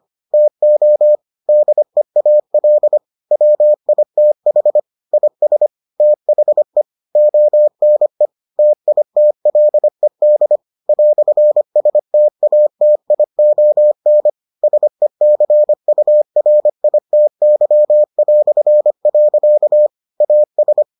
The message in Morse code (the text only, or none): TO DEAL WITH IS THE ONE TITLED "STATION SECURITY". AS